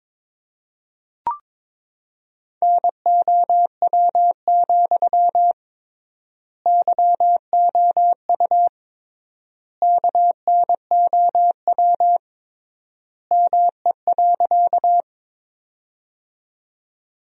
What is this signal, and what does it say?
Morse code, heard as a keyed tone, about 21 wpm, 700 Hz. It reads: NOW, YOU KNOW ME.
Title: 3x Word-Spacing - Sentences from Top 200 Words - 22wpm